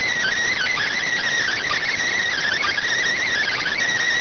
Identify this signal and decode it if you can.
Emergency Alert System (SAME) tones, heard as harsh buzzing bursts. It is heard from 0.4 to 3.7 s.